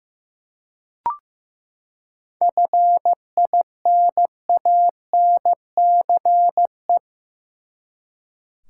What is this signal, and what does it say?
Morse code, heard as a keyed tone, about 15 wpm, 700 Hz. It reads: FINANCE
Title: Seven Letter Words - 15wpm